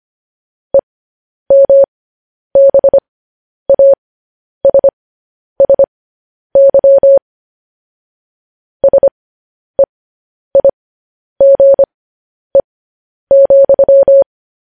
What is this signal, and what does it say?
Morse code, heard as a keyed tone, about 25 wpm, spaced as for 10 wpm, 550 Hz. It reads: EMBASSY SEIGE,
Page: news headlines in morse code at 10 wpm